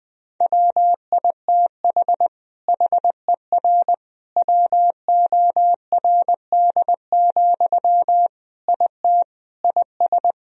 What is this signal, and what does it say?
Morse code, heard as a keyed tone, about 20 wpm, 700 Hz. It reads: WITH HER WORD, IT IS
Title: Sentences from Top 1000 Words - Review Part 2 - 20wpm